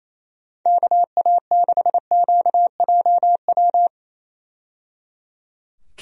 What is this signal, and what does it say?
Morse code, heard as a keyed tone, about 27 wpm, 700 Hz. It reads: KA6QJW
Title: USA Call Signs - 6 Characters - 28wpm